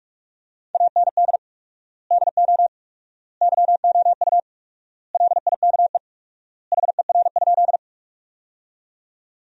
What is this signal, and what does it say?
Morse code, heard as a keyed tone, about 46 wpm, 700 Hz. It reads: AND DO YOU LIKE HER?
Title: 4x Word-Spacing - Sentences from Top 100 Words - 45wpm